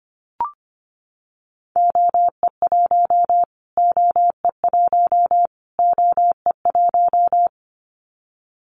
Morse code, heard as a keyed tone, about 25 wpm, 700 Hz. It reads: OE1 OE1 OE1
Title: Callsign Prefixes for Top 25 DXCC - Repeated 3x - 25wpm